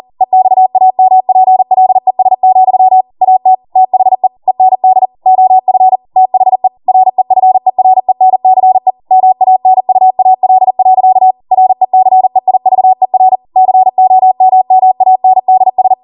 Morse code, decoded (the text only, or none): EXAMPLES, AT THE END OF THE REFERENCE MANUAL. RECEIVER COMMANDS